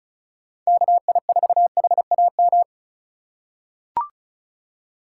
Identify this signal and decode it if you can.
Morse code, heard as a keyed tone, about 36 wpm, 700 Hz. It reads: KI4HAM
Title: USA Call Signs - 6 Characters - 4x Character Spacing - 35wpm